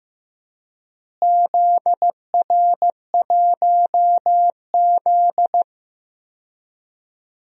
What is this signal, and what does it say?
Morse code, heard as a keyed tone, about 15 wpm, 700 Hz. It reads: ZR1Z
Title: DXCC-100 Call Signs - 4 Characters - Repeated 3x - 15wpm